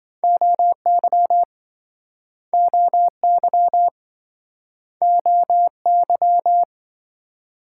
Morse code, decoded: OY OY OY

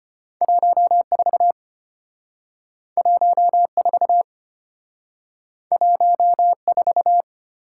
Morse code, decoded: 14 14 14

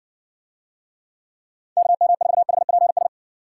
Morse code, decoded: KM4HPS

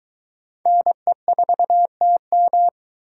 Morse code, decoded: NE4TM

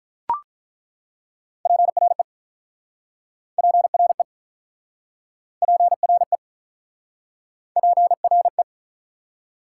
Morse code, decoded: PRE PRE PRE PRE

